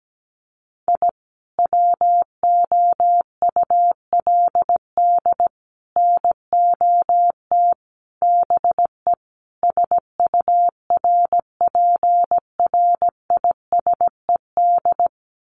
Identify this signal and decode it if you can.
Morse code, heard as a keyed tone, about 17 wpm, 700 Hz. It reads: I WOULD NOT BE SURPRISED
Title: Sets of 5 Words Encore2 17wpm